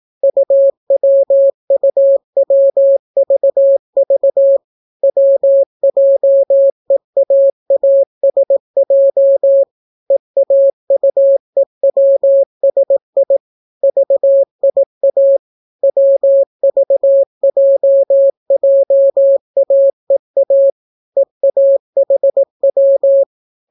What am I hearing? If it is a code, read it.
Morse code, heard as a keyed tone, about 18 wpm, 550 Hz. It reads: UWUWVV WJEAASJ EAUEWSI VIA WVJJAEA EAHW